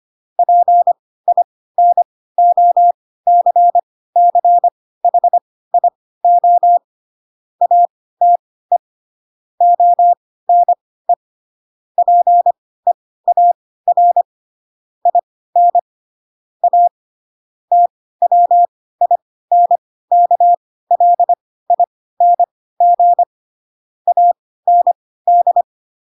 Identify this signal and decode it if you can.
Morse code, heard as a keyed tone, about 25 wpm, spaced as for 16 wpm, 700 Hz. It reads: PINOCCHIO ATE ONE PEAR IN A TWINKLING AND
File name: Pinocchio / Pinocchio ch7 10 25 ik1hge